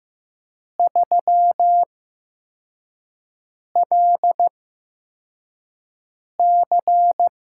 Morse code, heard as a keyed tone, about 15 wpm, spaced as for 4 wpm, 700 Hz. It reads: 3LC
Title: USA Call Signs - 4 Characters - 8x Character Spacing - 15wpm